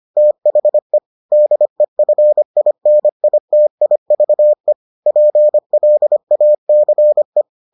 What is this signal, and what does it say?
Morse code, heard as a keyed tone, about 25 wpm, 600 Hz. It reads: THE DEFINITIVE PLACE